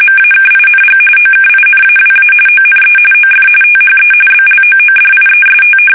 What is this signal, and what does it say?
RTTY, about 45.45 baud. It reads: MAKING OKINAWA ONE OF THE BLOODIEST
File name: fsk850